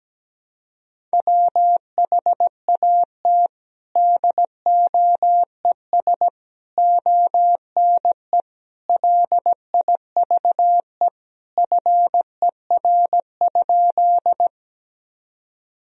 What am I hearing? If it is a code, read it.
Morse code, heard as a keyed tone, about 17 wpm, 700 Hz. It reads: WHAT DOES ONE LIVE FER?
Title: Sentences from Top 200 Words and Abbreviations 17wpm